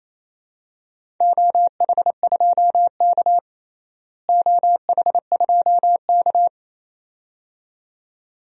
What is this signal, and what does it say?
Morse code, heard as a keyed tone, about 28 wpm, 700 Hz. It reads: OH2K OH2K
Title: DXCC-100 Call Signs - 4 Characters - Repeated 2x - 28wpm